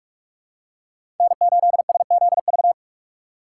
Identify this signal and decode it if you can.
Morse code, heard as a keyed tone, about 42 wpm, 700 Hz. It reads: N8SZV
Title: USA Call Signs - 5 Characters - 2x Character Spacing - 45wpm